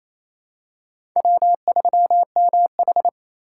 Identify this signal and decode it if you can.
Morse code, heard as a keyed tone, about 28 wpm, 700 Hz. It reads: W3MH
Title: USA Call Signs - 4 Characters - 3x Character Spacing - 28wpm